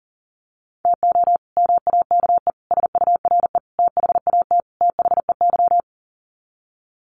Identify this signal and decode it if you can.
Morse code, heard as a keyed tone, about 40 wpm, 700 Hz. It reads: TO MAKE SURE THAT THEY
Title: Sets of 5 Words Encore2 40wpm